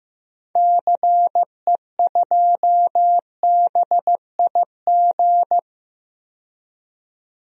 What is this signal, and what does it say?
Morse code, heard as a keyed tone, about 16 wpm, 700 Hz. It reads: CE2BIG
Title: DXCC-100 Call Signs - 6 Characters - 15wpm